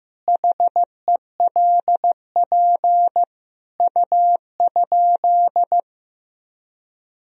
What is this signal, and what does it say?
Morse code, heard as a keyed tone, about 16 wpm, 700 Hz. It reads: HELP U?
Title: Sentences from Top 200 Words and Abbreviations 15wpm